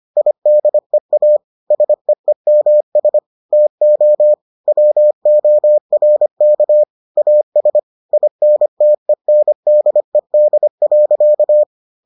Morse code, heard as a keyed tone, about 25 wpm, 600 Hz. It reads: IDEA SEEMS TO WORK AS INTENDED.